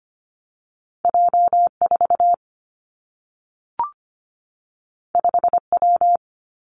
Morse code, heard as a keyed tone, about 25 wpm, 700 Hz. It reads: J4 5W
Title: Two Challenging Letter-Number Combinations 25wpm